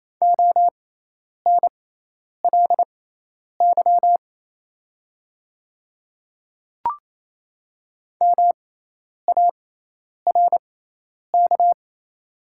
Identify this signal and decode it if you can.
Morse code, heard as a keyed tone, about 27 wpm, spaced as for 10 wpm, 690 Hz. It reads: ONLY MARK
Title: 6x Character Spacing - Top 300 Words - 28wpm